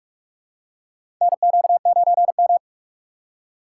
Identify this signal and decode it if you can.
Morse code, heard as a keyed tone, about 46 wpm, 700 Hz. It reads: NQ9M
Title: USA Call Signs - 4 Characters - 3x Character Spacing - 45wpm